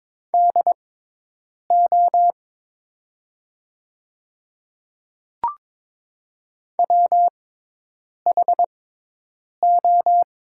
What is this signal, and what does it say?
Morse code, heard as a keyed tone, about 22 wpm, spaced as for 8 wpm, 700 Hz. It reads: DO WHO
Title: 6x Character Spacing - Top 100 Words - 22wpm